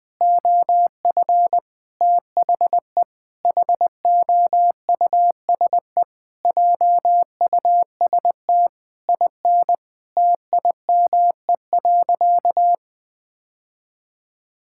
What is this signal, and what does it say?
Morse code, heard as a keyed tone, about 20 wpm, 700 Hz. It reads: OF THE HOUSE JUST IN TIME.
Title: Sentences from Top 1600 Words 20wpm